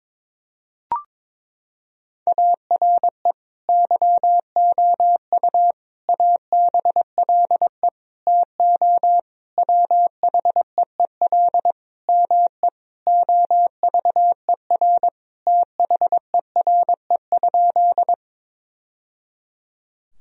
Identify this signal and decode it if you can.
Morse code, heard as a keyed tone, about 22 wpm, 700 Hz. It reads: ARE YOU ABLE TO WHEEL ME OVER THERE?
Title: Sentences from Top 1000 Words - Review Part 3 - 22wpm